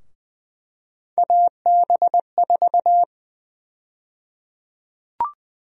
Morse code, heard as a keyed tone, about 20 wpm, 700 Hz. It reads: AB4
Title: USA Call Sign Prefixes - Repeated 2x - 20wpm